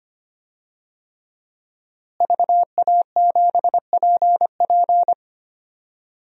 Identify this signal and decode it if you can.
Morse code, heard as a keyed tone, about 25 wpm, 700 Hz. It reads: VA7PP
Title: DXCC-100 Call Signs - 5 Characters - Repeated 3x - 25wpm